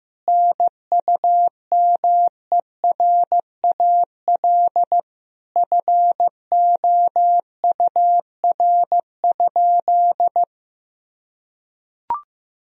Morse code, heard as a keyed tone, about 15 wpm, 700 Hz. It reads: NUMERAL FOUR?
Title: Sentences from Top 1000 Words - Review Part 2 - 15wpm